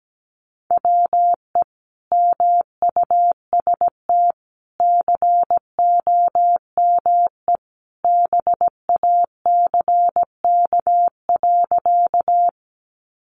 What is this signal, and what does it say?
Morse code, heard as a keyed tone, about 17 wpm, 700 Hz. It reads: WE MUST COME BACK.
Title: Sentences from Top 200 Words 17wpm